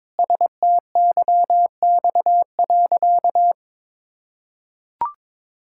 Morse code, heard as a keyed tone, about 22 wpm, 700 Hz. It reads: STYX.
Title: Pangram Challenge v2 - 22wpm